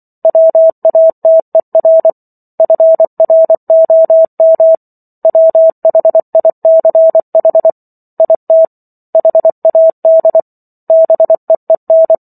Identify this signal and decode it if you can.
Morse code, heard as a keyed tone, about 24 wpm, 650 Hz. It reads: WATER FROM WHICH IT HAD BEEN